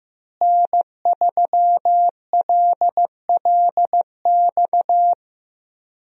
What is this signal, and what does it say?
Morse code, heard as a keyed tone, about 15 wpm, 700 Hz. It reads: N3LLX
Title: USA Call Signs - 5 Characters - Repeated 3x - 15wpm